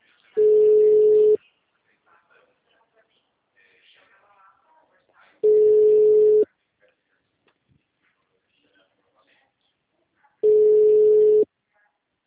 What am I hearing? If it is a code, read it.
Ringback tone (Europe, 425 Hz), with speech.